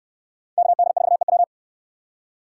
Morse code, heard as a keyed tone, about 69 wpm, 700 Hz. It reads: KD4EW